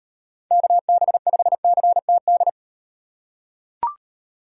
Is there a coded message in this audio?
Yes: Morse code from 0.5 to 2.5 s, heard as a keyed tone, about 38 wpm, 700 Hz. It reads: KB5CTD